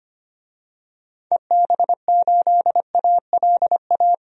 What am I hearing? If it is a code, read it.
Morse code, heard as a keyed tone, about 25 wpm, 700 Hz. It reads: EB8ALA
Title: DXCC-100 Call Signs - 6 Characters - 25wpm